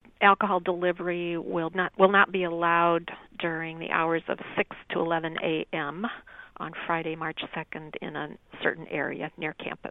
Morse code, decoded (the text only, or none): none